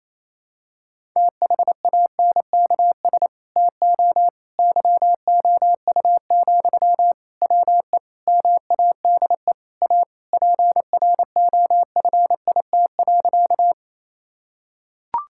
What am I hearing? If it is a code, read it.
Morse code, heard as a keyed tone, about 28 wpm, 700 Hz. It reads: THANKS TO YOU, WE MADE A PROFIT.
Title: Sentences from Top 1000 Words - Review Part 5 - 28wpm